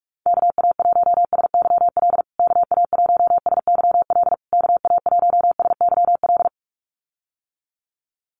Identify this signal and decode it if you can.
Morse code, heard as a keyed tone, about 44 wpm, 700 Hz. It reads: KA1SYL KA1SYL KA1SYL
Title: USA Call Signs Repeated 3x 45wpm